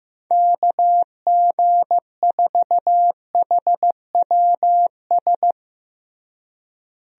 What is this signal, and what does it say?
Morse code, heard as a keyed tone, about 15 wpm, 700 Hz. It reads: KG4HWS
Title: USA Call Signs - 6 Characters - Repeated 2x - 15wpm